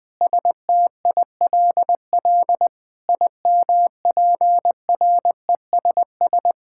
Morse code, heard as a keyed tone, about 20 wpm, 700 Hz. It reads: STILL IMPRESS
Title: Sentences from Top 1600 Words 20wpm